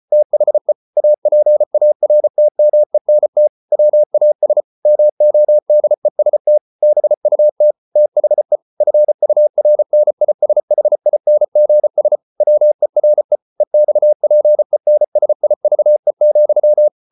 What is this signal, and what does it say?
Morse code, heard as a keyed tone, about 34 wpm, 600 Hz. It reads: THE APARTMENT WAS MODEST BUT THE FURNISHINGS WERE EXPENSIVE,